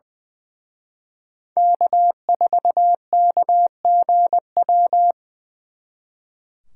Morse code, heard as a keyed tone, about 23 wpm, 700 Hz. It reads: K4KGW
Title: Speed Racing - USA 1x3 Call Signs 20wpm